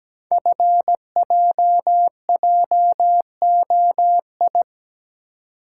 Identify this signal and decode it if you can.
Morse code, heard as a keyed tone, about 17 wpm, 700 Hz. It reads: FJJOI
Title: Code Groups - 5 Letters - 17wpm